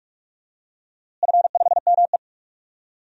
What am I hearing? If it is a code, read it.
Morse code, heard as a keyed tone, about 46 wpm, 700 Hz. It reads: F5ME
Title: DXCC-100 Call Signs - 4 Characters - Repeated 3x - 45wpm